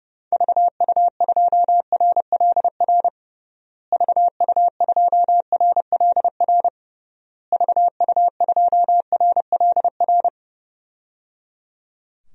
Morse code, heard as a keyed tone, about 32 wpm, 700 Hz. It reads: VU2RLR VU2RLR VU2RLR
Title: DXCC-100 Call Signs - 6 Characters - Repeated 3x - 30wpm